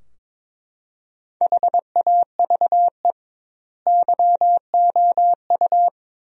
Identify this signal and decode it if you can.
Morse code, heard as a keyed tone, about 22 wpm, 700 Hz. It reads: HAVE YOU
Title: 2x Word-Spacing - Sentences from Top 400 Words - 22wpm